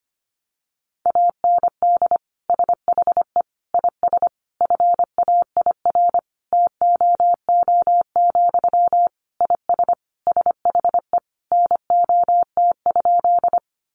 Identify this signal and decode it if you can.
Morse code, heard as a keyed tone, about 25 wpm, 700 Hz. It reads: AND SHE IS FAIR TOO, IS SHE NOT?
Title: Sentences from Top 1300 Words 25wpm